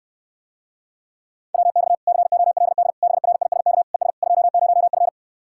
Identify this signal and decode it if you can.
Morse code, heard as a keyed tone, about 69 wpm, 700 Hz. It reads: WX COLD BRISK ES 20F